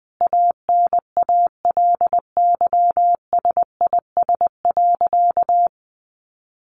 Morse code, heard as a keyed tone, about 20 wpm, 700 Hz. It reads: ANALYSIS.
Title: Sentences from Top 1400 Words 20wpm